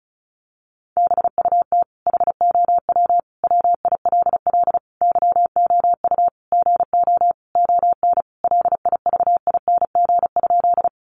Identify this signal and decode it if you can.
Morse code, heard as a keyed tone, about 34 wpm, 700 Hz. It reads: BUT HOW WILL YOU GO ON LIVING?
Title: Sentences from Top 1100 Words 35wpm